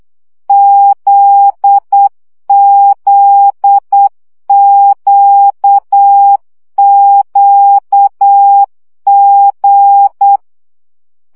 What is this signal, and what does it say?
Morse code, heard as a keyed tone, about 8 wpm, 800 Hz. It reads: ZZQQG
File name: CW gyakorlas g z q 7 baud